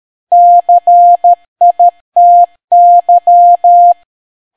Morse code, heard as a keyed tone, about 13 wpm, 700 Hz. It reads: CITY